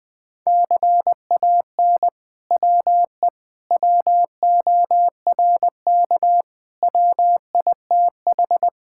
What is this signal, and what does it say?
Morse code, heard as a keyed tone, about 20 wpm, 700 Hz. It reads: CAN WE WORK WITH
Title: Sentences from Top 1000 Words - Review Part 5 - 20wpm